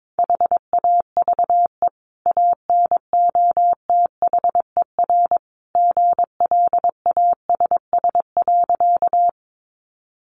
Morse code, heard as a keyed tone, about 22 wpm, 700 Hz. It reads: HAVE ANOTHER GLASS.
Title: Sentences from Top 1300 Words 22wpm